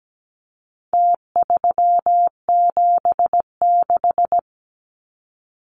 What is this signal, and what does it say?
Morse code, heard as a keyed tone, about 17 wpm, 700 Hz. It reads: T376
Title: Numbers - Contest Style - 4-Digit 17wpm